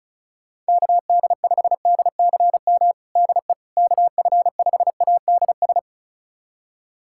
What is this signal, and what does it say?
Morse code, heard as a keyed tone, about 36 wpm, 700 Hz. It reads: KD5DCM DE KF5ADS
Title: USA Callsign DE Callsign 35wpm